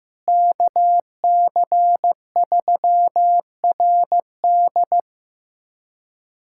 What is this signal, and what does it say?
Morse code, heard as a keyed tone, about 16 wpm, 700 Hz. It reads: KC3RD
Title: USA Callsign DE Callsign 15wpm